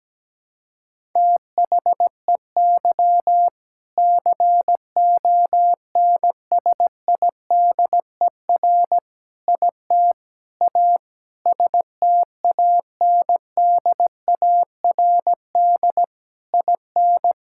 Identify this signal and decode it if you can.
Morse code, heard as a keyed tone, about 17 wpm, 700 Hz. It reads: THEY CONSIDER IT A STANDARD IN
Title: Sentences from Top 1000 Words - Review Part 6 - 17wpm